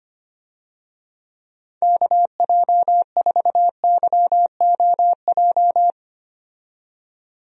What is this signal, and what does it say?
Morse code, heard as a keyed tone, about 25 wpm, 700 Hz. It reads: KJ4YOJ
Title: USA Call Signs - 6 Characters - 25wpm